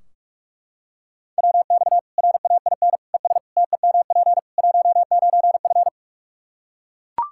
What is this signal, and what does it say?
Morse code, heard as a keyed tone, about 42 wpm, 700 Hz. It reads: WX RAIN ES TEMP 19F
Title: Practice QSO - Exchange 2 - Info - 45wpm